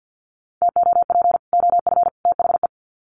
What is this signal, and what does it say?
Morse code, heard as a keyed tone, about 50 wpm, 700 Hz. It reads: TOP OF THE
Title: Sets of 3 Words 50wpm